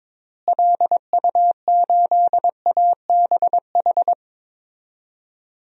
Morse code, heard as a keyed tone, about 22 wpm, 700 Hz. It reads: LU8ABH
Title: DXCC-100 Call Signs - 6 Characters - Repeated 2x - 22wpm